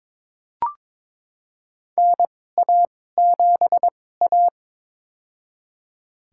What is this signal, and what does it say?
Morse code, heard as a keyed tone, about 22 wpm, spaced as for 16 wpm, 700 Hz. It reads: NA7A
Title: USA Call Signs - 4 Characters - 2x Character Spacing - 22wpm